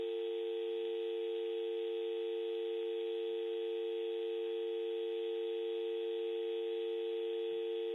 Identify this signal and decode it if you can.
Dial tone (UK), 350 + 450 Hz, with noise.